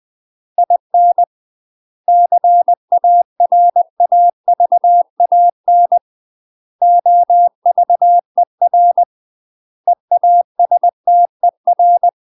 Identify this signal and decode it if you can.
Morse code, heard as a keyed tone, about 20 wpm, 700 Hz. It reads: IN CARAVAN OVER EASTER
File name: UK MASTER QSOs TRAINING / 20WPM / UK MASTER QSOs 0062